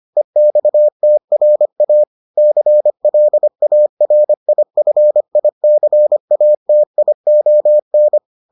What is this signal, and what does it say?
Morse code, heard as a keyed tone, about 25 wpm, 600 Hz. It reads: EXTRA CLARIFICATION